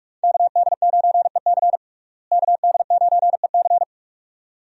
Morse code, heard as a keyed tone, about 46 wpm, 700 Hz. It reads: KD9EC KD9EC
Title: USA Call Signs - 5 Characters - Repeated 2x - 45wpm